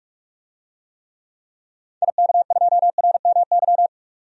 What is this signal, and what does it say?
Morse code, heard as a keyed tone, about 47 wpm, 700 Hz. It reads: IK2RMY
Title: DXCC-100 Call Signs - 6 Characters - 45wpm